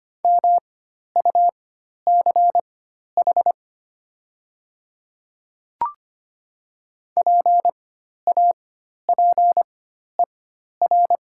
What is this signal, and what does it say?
Morse code, heard as a keyed tone, about 25 wpm, spaced as for 12 wpm, 700 Hz. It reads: MUCH PAPER
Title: 4x Character Spacing - Top 300 Words - 25wpm